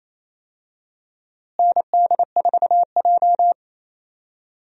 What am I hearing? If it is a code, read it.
Morse code, heard as a keyed tone, about 28 wpm, 700 Hz. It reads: ND4J
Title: USA Call Signs - 4 Characters - Repeated 3x - 28wpm